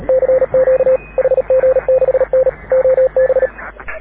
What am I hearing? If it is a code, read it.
Morse code, heard as a keyed tone, about 36 wpm, 530 Hz. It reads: CQ HG6N OB
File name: AUDIO2019 / HG6N CW SSB MIX